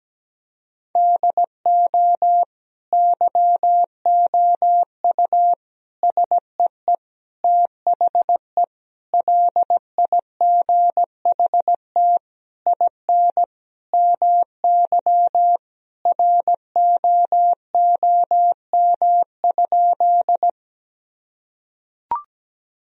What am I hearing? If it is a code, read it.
Morse code, heard as a keyed tone, about 17 wpm, 700 Hz. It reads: DO YOU SEE THE LIGHT IN MY ROOM?
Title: Sentences from Top 1000 Words - Review Part 1 - 17wpm